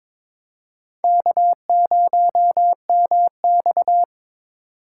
Morse code, heard as a keyed tone, about 22 wpm, 700 Hz. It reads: K0MX
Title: USA Call Signs - 4 Characters - 3x Character Spacing - 22wpm